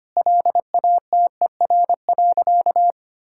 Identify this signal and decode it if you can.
Morse code, heard as a keyed tone, about 25 wpm, 700 Hz. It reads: LATER.
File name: practice / 25wpm / Sentences with 4-Word Phrases - 25wpm